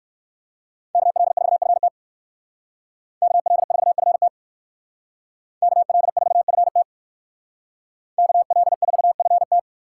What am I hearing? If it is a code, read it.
Morse code, heard as a keyed tone, about 58 wpm, 700 Hz. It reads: KL4FT KL4FT KL4FT KL4FT